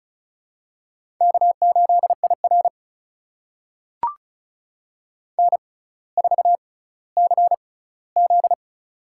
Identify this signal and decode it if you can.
Morse code, heard as a keyed tone, about 36 wpm, spaced as for 18 wpm, 700 Hz. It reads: K8IR N4CZ
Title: USA Call Signs - 4 Characters - 6x Character Spacing - 35wpm